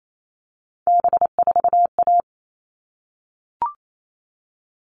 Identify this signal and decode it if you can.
Morse code, heard as a keyed tone, about 28 wpm, 700 Hz. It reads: B4A
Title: All Cut Numbers - 3-Digit 28wpm